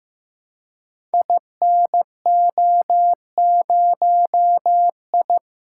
Morse code, heard as a keyed tone, about 15 wpm, 700 Hz. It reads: INO0I
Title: Code Groups - 5 Letter:Numbers - 15wpm